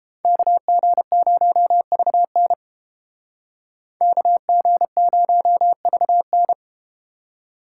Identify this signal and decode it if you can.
Morse code, heard as a keyed tone, about 32 wpm, 700 Hz. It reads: KG0VN KG0VN